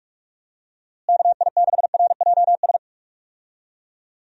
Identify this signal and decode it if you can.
Morse code, heard as a keyed tone, about 41 wpm, 700 Hz. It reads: KI6RJS
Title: USA Call Signs - 6 Characters - Repeated 2x - 45wpm